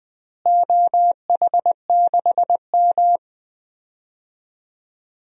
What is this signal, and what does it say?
Morse code, heard as a keyed tone, about 20 wpm, 700 Hz. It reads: OH6M
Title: DXCC-100 Call Signs - 4 Characters - Repeated 2x - 20wpm